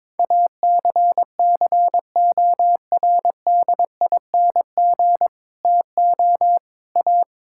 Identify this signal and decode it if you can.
Morse code, heard as a keyed tone, about 21 wpm, 700 Hz. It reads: ACCORDING TO A
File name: Sentences with 7-Word Phrases - 22wpm